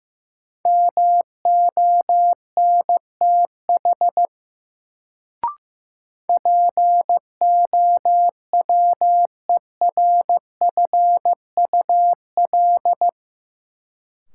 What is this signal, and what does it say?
Morse code, heard as a keyed tone, about 15 wpm, 700 Hz. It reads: MONTH POWERFUL
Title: Sets of 5 Words Encore - New Word Drill - 15wpm